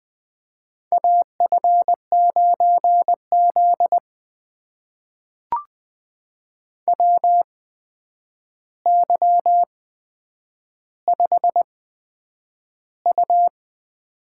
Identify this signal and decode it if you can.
Morse code, heard as a keyed tone, about 20 wpm, spaced as for 9 wpm, 700 Hz. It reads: AF9Z WY5U
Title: USA Call Signs - 4 Characters - 8x Character Spacing - 20wpm